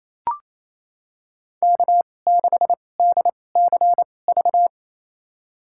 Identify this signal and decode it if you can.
Morse code, heard as a keyed tone, about 28 wpm, spaced as for 20 wpm, 700 Hz. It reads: K6DCV
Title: USA Call Signs - 5 Characters - 2x Character Spacing - 28wpm